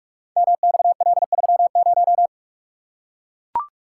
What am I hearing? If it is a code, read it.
Morse code, heard as a keyed tone, about 44 wpm, 700 Hz. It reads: MXL30
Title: Code Groups - 5 Letter:Numbers - 45wpm